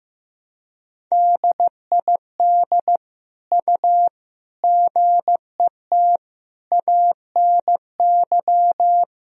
Morse code, heard as a keyed tone, about 15 wpm, 700 Hz. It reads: DID U GET ANY
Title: Sentences from Top 400 Words and Abbreviations 15wpm